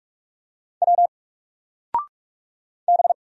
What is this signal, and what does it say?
Morse code, heard as a keyed tone, about 44 wpm, 700 Hz. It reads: W B
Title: Single Letter-Number Alt 45wpm